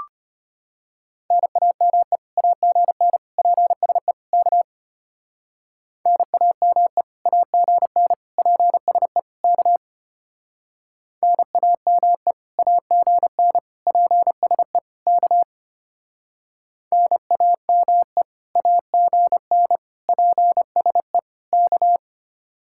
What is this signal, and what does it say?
Morse code, heard as a keyed tone, about 30 wpm, 700 Hz. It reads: NAME AGN PSE K NAME AGN PSE K NAME AGN PSE K NAME AGN PSE K
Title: Speed-Racing Common QSO Phrases Encore - 25wpm